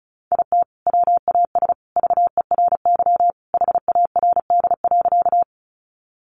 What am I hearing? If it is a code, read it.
Morse code, heard as a keyed tone, about 34 wpm, 700 Hz. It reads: IT WAS VERY HARD.
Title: Sentences from Top 300 Words 35wpm